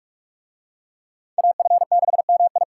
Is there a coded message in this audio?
Yes: Morse code from 1.4 to 2.6 s, heard as a keyed tone, about 41 wpm, 700 Hz. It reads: AF6MI